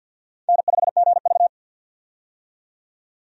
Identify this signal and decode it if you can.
Morse code, heard as a keyed tone, about 53 wpm, 700 Hz. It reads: N5GV
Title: USA Call Signs - 4 Characters - 50wpm